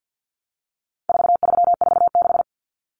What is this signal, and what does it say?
Morse code, heard as a keyed tone, about 50 wpm, 700 Hz. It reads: 4346
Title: Numbers - 4-Digit 50wpm